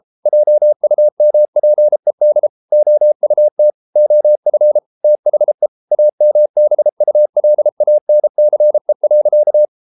Morse code, heard as a keyed tone, about 33 wpm, 600 Hz. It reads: JUMPED OUT OF THE AMBULANCE.